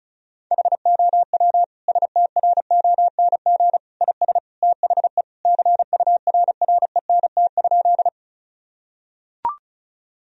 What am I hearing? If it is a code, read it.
Morse code, heard as a keyed tone, about 36 wpm, 700 Hz. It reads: HOW STRONG IS THE CURRENT?
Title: Sentences from Top 1000 Words - Review Part 4 - 35wpm